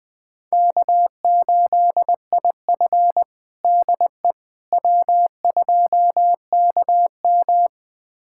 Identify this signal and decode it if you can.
Morse code, heard as a keyed tone, about 20 wpm, 700 Hz. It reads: K8IF DE W2KM
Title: USA Callsign DE Callsign - 4 Characters - 20wpm